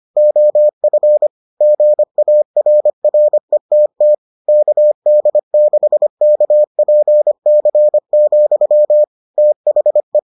Morse code, heard as a keyed tone, about 25 wpm, 600 Hz. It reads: OF GARRETT KD6KPC, THE